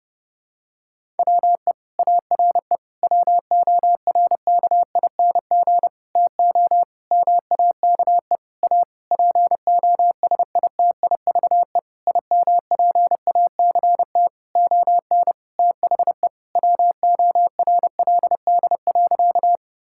Morse code, heard as a keyed tone, about 30 wpm, 700 Hz. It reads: WE ARE WORKING TO MAKE A POSITIVE IMPACT ON THE WORLD.
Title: Sentences from Top 1000 Words - Review Part 6 - 30wpm